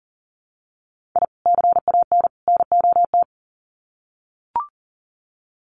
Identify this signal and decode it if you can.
Morse code, heard as a keyed tone, about 40 wpm, 700 Hz. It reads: I CAN NOT
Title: Sets of 3 Words 40wpm